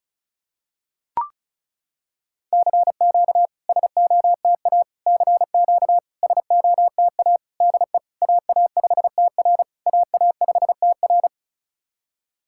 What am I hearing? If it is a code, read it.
Morse code, heard as a keyed tone, about 36 wpm, 690 Hz. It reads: CQ SOTA CQ SOTA DE AA5TR AA5TR
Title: SOTA QSO Practice 35wpm